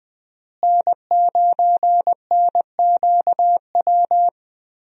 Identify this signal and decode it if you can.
Morse code, heard as a keyed tone, about 20 wpm, 700 Hz. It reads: N9NQW
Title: USA Call Signs - 5 Characters - Repeated 3x - 20wpm